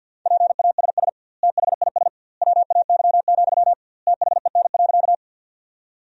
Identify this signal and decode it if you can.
Morse code, heard as a keyed tone, about 50 wpm, 700 Hz. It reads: PASS THIS WAY, THEN.
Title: 2x Word-Spacing - Sentences from Top 400 Words - 50wpm